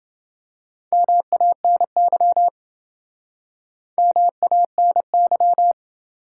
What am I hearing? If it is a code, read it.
Morse code, heard as a keyed tone, about 28 wpm, 700 Hz. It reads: MANY MANY